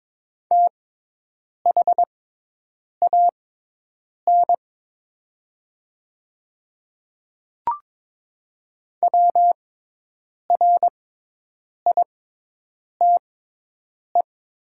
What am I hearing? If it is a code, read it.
Morse code, heard as a keyed tone, about 22 wpm, spaced as for 8 wpm, 700 Hz. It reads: THAN WRITE
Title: 6x Character Spacing - Top 100 Words - 22wpm